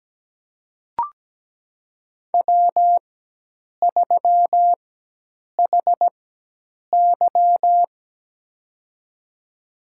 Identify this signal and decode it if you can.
Morse code, heard as a keyed tone, about 17 wpm, spaced as for 8 wpm, 700 Hz. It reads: W3HY
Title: USA Call Signs - 4 Characters - 4x Character Spacing - 17wpm